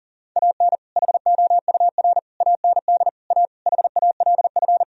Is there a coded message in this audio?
Yes: Morse code from 0.4 to 4.8 s, heard as a keyed tone, about 51 wpm, 700 Hz. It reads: AN HOUR AND A HALF